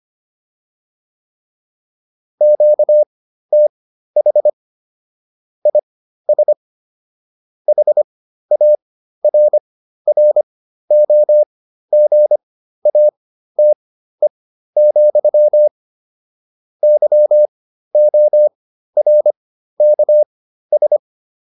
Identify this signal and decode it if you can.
Morse code, heard as a keyed tone, about 25 wpm, spaced as for 13 wpm, 600 Hz. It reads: QTH IS HARROGATE, YORKS